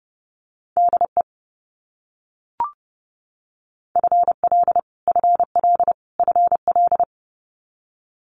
Morse code, heard as a keyed tone, about 30 wpm, 700 Hz. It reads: DE FL FL FL
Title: US States Repeated 3x 30wpm